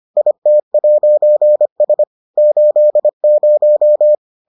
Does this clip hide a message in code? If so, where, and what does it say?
Yes: Morse code from 0.2 to 4.2 s, heard as a keyed tone, about 25 wpm, 600 Hz. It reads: IT'S 80